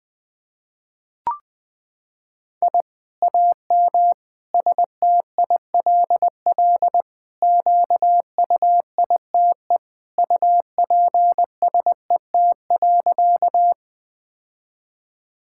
Morse code, heard as a keyed tone, about 20 wpm, 700 Hz. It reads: I AM STILL QUITE UPSET.
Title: Sentences from Top 1000 Words - Review Part 5 - 20wpm